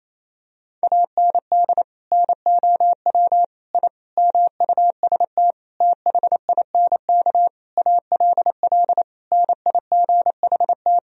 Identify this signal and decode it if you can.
Morse code, heard as a keyed tone, about 28 wpm, 700 Hz. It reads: AND NOW I MUST THINK ALL NIGHT
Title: Sentences from Top 1000 Words - Review Part 3 - 28wpm